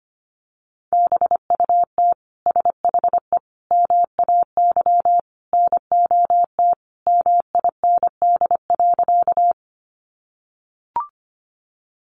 Morse code, heard as a keyed tone, about 25 wpm, 700 Hz. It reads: BUT SHE MAY NOT MIND.
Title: Sentences from Top 600 Words 25wpm